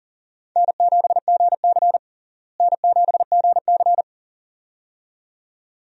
Morse code, heard as a keyed tone, about 40 wpm, 700 Hz. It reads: N7GC N7GC
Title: USA Call Signs - 4 Characters - Repeated 2x - 40wpm